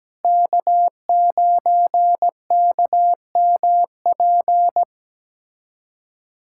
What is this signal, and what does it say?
Morse code, heard as a keyed tone, about 17 wpm, 700 Hz. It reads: K9KMP
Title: USA Callsign DE Callsign 17wpm